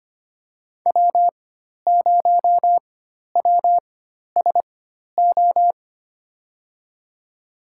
Morse code, heard as a keyed tone, about 25 wpm, spaced as for 12 wpm, 700 Hz. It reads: W0WSO
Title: USA Call Signs - 4x Character Spacing - 25wpm